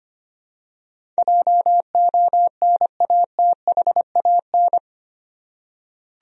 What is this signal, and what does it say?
Morse code, heard as a keyed tone, about 25 wpm, 700 Hz. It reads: JONATHAN
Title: USA Common Names - 25wpm